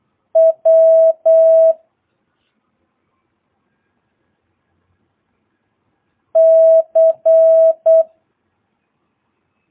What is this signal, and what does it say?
Morse code, heard as a keyed tone, about 8 wpm, 650 Hz. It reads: W C
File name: jh8cbh / 2022cw